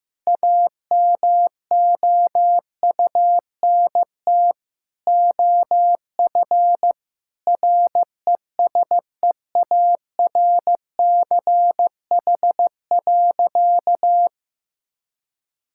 Morse code, heard as a keyed tone, about 15 wpm, 700 Hz. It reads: AMOUNT OF RESEARCH.
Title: Sentences from Top 1000 Words - Review Part 5 - 15wpm